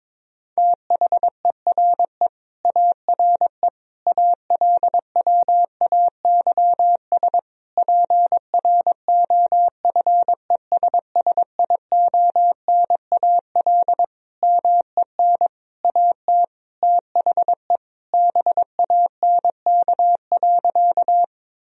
Morse code, heard as a keyed tone, about 22 wpm, 700 Hz. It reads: THERE ARE ALWAYS PROFESSIONAL MEN AT THE BANK.
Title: Sentences from Top 1000 Words - Review Part 4 - 22wpm